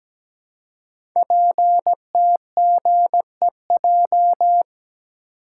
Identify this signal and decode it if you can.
Morse code, heard as a keyed tone, about 17 wpm, 700 Hz. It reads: PTGEJ
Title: Code Groups - 5 Letters - 17wpm